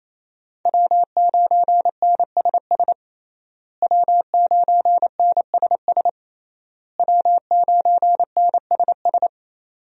Morse code, heard as a keyed tone, about 28 wpm, 700 Hz. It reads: W9NSS W9NSS W9NSS
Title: USA Call Signs - 5 Characters - Repeated 3x - 28wpm